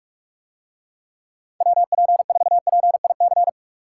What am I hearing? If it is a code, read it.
Morse code, heard as a keyed tone, about 46 wpm, 700 Hz. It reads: WP4PIC